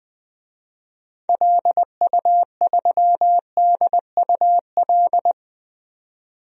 Morse code, heard as a keyed tone, about 21 wpm, 700 Hz. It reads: LU3DUL
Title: DXCC-100 Call Signs - 6 Characters - 20wpm